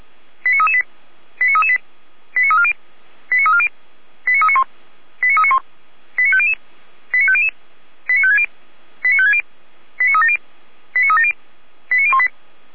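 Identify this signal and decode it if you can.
Five-tone selective call (ZVEI), heard as quick runs of five tones, tones of about 70 ms. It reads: 89338 89338 89340 89340 89391 89391 89500 89500 89560 89560 89370 89370 89018
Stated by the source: Five-tone ZVEI selective call received by SDR.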